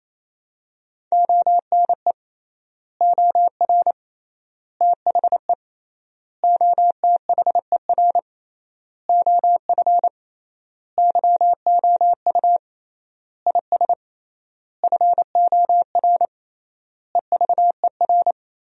Morse code, heard as a keyed tone, about 28 wpm, 700 Hz. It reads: ONE OR THE OTHER OF YOU IS FOR EVER
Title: 3x Word-Spacing - Sentences from Top 400 Words - 28wpm